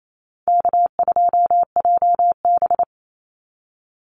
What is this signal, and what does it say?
Morse code, heard as a keyed tone, about 28 wpm, 700 Hz. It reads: K2JB
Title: NA SOTA Activator Callsigns 28wpm